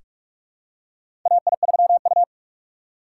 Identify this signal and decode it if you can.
Morse code, heard as a keyed tone, about 46 wpm, 700 Hz. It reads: AI3U
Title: USA Call Signs - 4 Characters - 6x Character Spacing - 45wpm